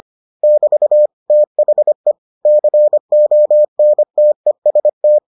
Morse code, heard as a keyed tone, about 25 wpm, 600 Hz. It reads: <BT> THE CONTEST